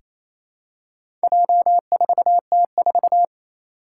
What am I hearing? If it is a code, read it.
Morse code, heard as a keyed tone, about 27 wpm, 700 Hz. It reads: J4T4
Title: Four Challenging Letter-Number Combinations 28wpm